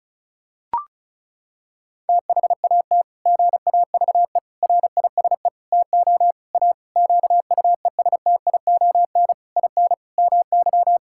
Morse code, heard as a keyed tone, about 36 wpm, 700 Hz. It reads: THAT GAVE RISE TO A QUESTION IN MY
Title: Sentences from Top 1000 Words - Review Part 6 - 35wpm